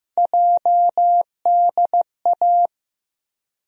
Morse code, heard as a keyed tone, about 16 wpm, 700 Hz. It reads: JDA
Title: USA Call Sign Postfixes - 3 Letters 15wpm